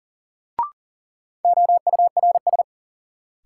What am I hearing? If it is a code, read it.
Morse code, heard as a keyed tone, about 40 wpm, 700 Hz. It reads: OURS
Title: Sentences from Top 1300 Words - New Word Drill Alt - 40wpm